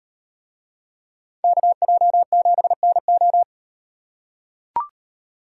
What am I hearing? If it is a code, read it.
Morse code, heard as a keyed tone, about 38 wpm, 700 Hz. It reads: KJ7NO